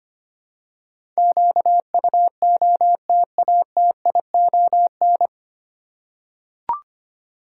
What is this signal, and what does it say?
Morse code, heard as a keyed tone, about 25 wpm, 700 Hz. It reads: QUOTATION
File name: practice / 25wpm / 9 Letter Words - First 3 Letters Given - 25wpm